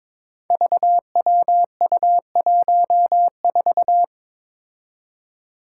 Morse code, heard as a keyed tone, about 22 wpm, 700 Hz. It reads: VWU14
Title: Five Challenging Letter-Number Combinations 22wpm